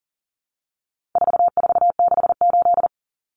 Morse code, heard as a keyed tone, about 40 wpm, 700 Hz. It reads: 4468
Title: Numbers - 4-Digit 40wpm